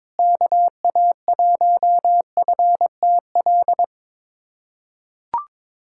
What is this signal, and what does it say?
Morse code, heard as a keyed tone, about 22 wpm, 700 Hz. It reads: KA1FTL
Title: USA Callsign DE Callsign - 6 Characters - 22wpm